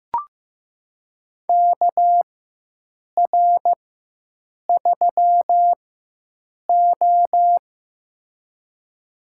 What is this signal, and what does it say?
Morse code, heard as a keyed tone, about 15 wpm, spaced as for 7 wpm, 700 Hz. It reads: KR3O
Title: USA Call Signs - 4 Characters - 4x Character Spacing - 15wpm